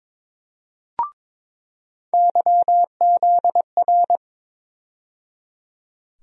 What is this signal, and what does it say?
Morse code, heard as a keyed tone, about 22 wpm, 700 Hz. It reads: YZR
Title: USA Call Sign Postfixes - 3 Letters 22wpm